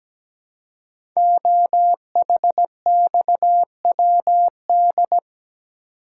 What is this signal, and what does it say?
Morse code, heard as a keyed tone, about 17 wpm, 700 Hz. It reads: OHXWD
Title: Code Groups - 5 Letters - 17wpm